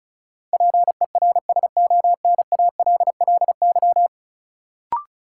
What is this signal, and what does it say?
Morse code, heard as a keyed tone, about 34 wpm, 700 Hz. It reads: PERSONALLY
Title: Sentences from Top 1400 Words - New Word Drill Alt - 35wpm